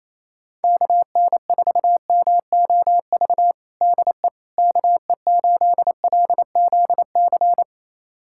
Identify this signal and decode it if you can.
Morse code, heard as a keyed tone, about 27 wpm, 700 Hz. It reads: KN4MOV DE KE8LZC
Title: USA Callsign DE Callsign - 6 Characters - 28wpm